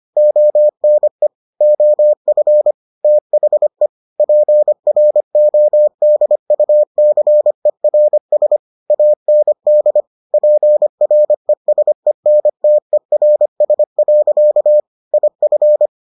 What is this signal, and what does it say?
Morse code, heard as a keyed tone, about 25 wpm, 600 Hz. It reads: ONE OF THE PRODUCERS AND PRESENTERS. IF